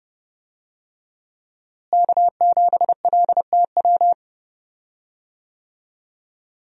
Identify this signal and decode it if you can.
Morse code, heard as a keyed tone, about 30 wpm, 700 Hz. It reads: K7LTW
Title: USA Call Signs - 5 Characters - Repeated 2x - 30wpm